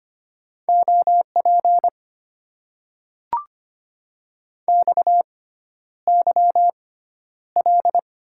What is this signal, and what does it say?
Morse code, heard as a keyed tone, about 25 wpm, spaced as for 11 wpm, 700 Hz. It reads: OP XYL
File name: Top 90 QSO Elements - 6x Character Spacing - 25wpm